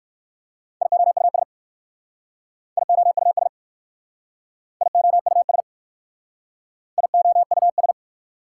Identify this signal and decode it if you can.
Morse code, heard as a keyed tone, about 54 wpm, 700 Hz. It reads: IOUS IOUS IOUS IOUS
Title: Speed-Racing Prefixes and Suffixes 45wpm